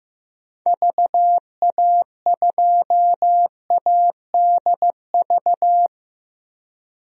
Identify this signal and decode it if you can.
Morse code, heard as a keyed tone, about 15 wpm, 700 Hz. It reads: VA2ADV
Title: DXCC-100 Call Signs - 6 Characters - Repeated 3x - 15wpm